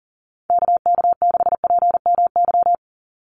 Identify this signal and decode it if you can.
Morse code, heard as a keyed tone, about 40 wpm, 700 Hz. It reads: KK6PMY
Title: USA Call Signs 40wpm